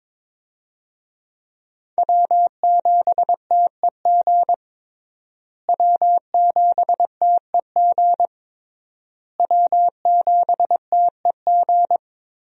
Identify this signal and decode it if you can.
Morse code, heard as a keyed tone, about 22 wpm, 700 Hz. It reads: W7TEG W7TEG W7TEG
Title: USA Call Signs - 5 Characters - Repeated 3x - 22wpm